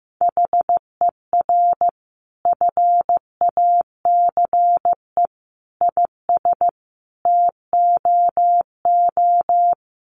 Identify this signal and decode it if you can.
Morse code, heard as a keyed tone, about 14 wpm, 700 Hz. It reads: HER FACE IS TOO
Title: Sentences from Top 1200 Words 15wpm